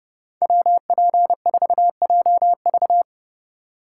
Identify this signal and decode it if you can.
Morse code, heard as a keyed tone, about 31 wpm, 700 Hz. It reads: WP4JV